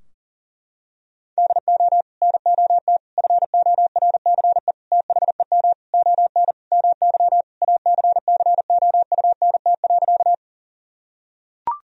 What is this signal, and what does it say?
Morse code, heard as a keyed tone, about 40 wpm, 700 Hz. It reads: DO NOT FORCE THEM ON MY ACCOUNT.
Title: Sentences from Top 1000 Words - Review Part 3 - 40wpm